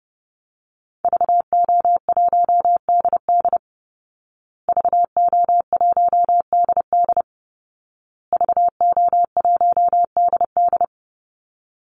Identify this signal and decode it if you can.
Morse code, heard as a keyed tone, about 30 wpm, 700 Hz. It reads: VO1DD VO1DD VO1DD
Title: Call Signs Repeated 2x - 30wpm